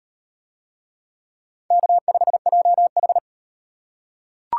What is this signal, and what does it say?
Morse code, heard as a keyed tone, about 38 wpm, 700 Hz. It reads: K5JH